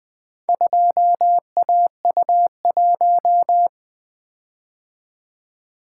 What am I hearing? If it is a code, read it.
Morse code, heard as a keyed tone, about 20 wpm, 700 Hz. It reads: 2AU1
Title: Four Letter-Number Combinations - 20wpm